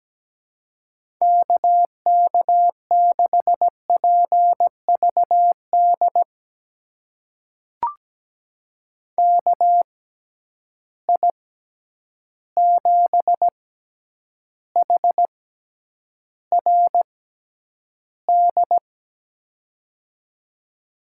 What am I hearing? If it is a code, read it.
Morse code, heard as a keyed tone, about 17 wpm, spaced as for 9 wpm, 700 Hz. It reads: KK6PVD KI7HRD
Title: USA Call Signs - 6x Character Spacing - 17wpm